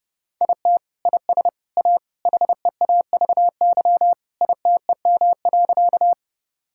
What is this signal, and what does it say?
Morse code, heard as a keyed tone, about 30 wpm, 700 Hz. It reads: IT IS A HEAVY ITEM.
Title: Sentences from Top 1000 Words - Review Part 6 - 30wpm